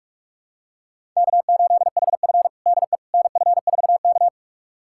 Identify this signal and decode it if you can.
Morse code, heard as a keyed tone, about 41 wpm, 700 Hz. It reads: K8HF DE NF4K
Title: USA Callsign DE Callsign - 4 Characters - 45wpm